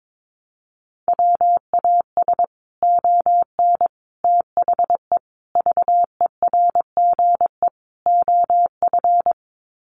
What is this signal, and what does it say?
Morse code, heard as a keyed tone, about 22 wpm, 700 Hz. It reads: WAS ON THE VERGE OF
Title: Sets of 5 Words Encore 22wpm